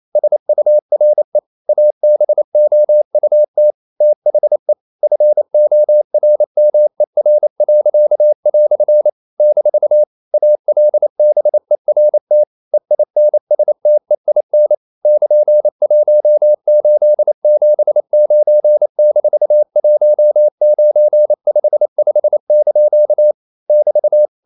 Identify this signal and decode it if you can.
Morse code, heard as a keyed tone, about 28 wpm, 600 Hz. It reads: SURE ABOUT THE FORMER." - ALBERT EINSTEIN <KN>1879-1955) <BT>
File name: CW / Practice / EnglishQuotations0128